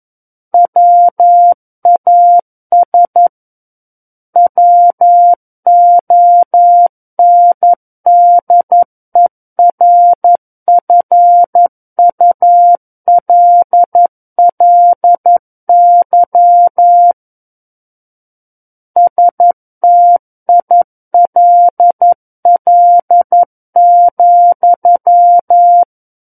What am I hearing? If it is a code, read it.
Morse code, heard as a keyed tone, about 11 wpm, 700 Hz. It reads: WAS WONDERFULLY STILL,
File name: Book One / 02-Chapter 2 - 11 WPM